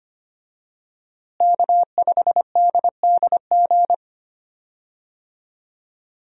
Morse code, heard as a keyed tone, about 25 wpm, 700 Hz. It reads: K5DDG
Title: USA Call Signs - 5 Characters - Repeated 2x - 25wpm